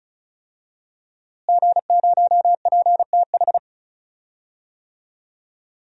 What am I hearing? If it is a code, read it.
Morse code, heard as a keyed tone, about 36 wpm, 700 Hz. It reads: G0PTH
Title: DXCC-100 Call Signs - 5 Characters - Repeated 2x - 35wpm